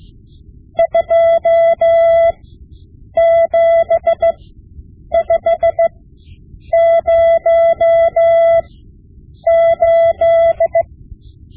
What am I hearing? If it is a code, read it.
Morse code, heard as a keyed tone, about 16 wpm, spaced as for 8 wpm, 670 Hz. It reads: 27508
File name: Morse Lessons / Morse Lessons / Tape-5B